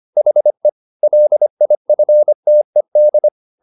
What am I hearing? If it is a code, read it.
Morse code, heard as a keyed tone, about 25 wpm, 600 Hz. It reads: HE LIFTED